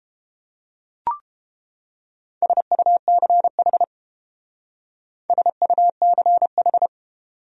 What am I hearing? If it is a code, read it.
Morse code, heard as a keyed tone, about 30 wpm, 700 Hz. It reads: SUCH SUCH